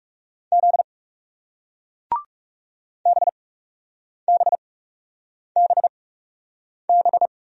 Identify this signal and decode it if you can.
Morse code, heard as a keyed tone, about 39 wpm, spaced as for 8 wpm, 700 Hz. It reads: Z BBBB